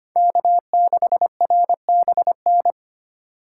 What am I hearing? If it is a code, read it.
Morse code, heard as a keyed tone, about 31 wpm, 700 Hz. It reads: K6RBN